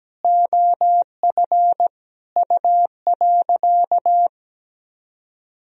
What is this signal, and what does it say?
Morse code, heard as a keyed tone, about 17 wpm, 700 Hz. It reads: OF U.
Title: Sentences from Top 400 Words and Abbreviations 17wpm